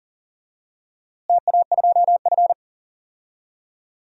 Morse code, heard as a keyed tone, about 37 wpm, 700 Hz. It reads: TA2F
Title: DXCC-100 Call Signs - 4 Characters - Repeated 2x - 40wpm